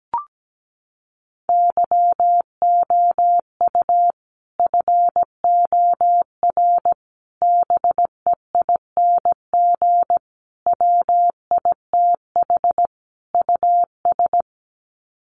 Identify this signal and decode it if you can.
Morse code, heard as a keyed tone, about 17 wpm, 700 Hz. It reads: YOU FOR BEING WITH US
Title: Sets of 5 Words 17wpm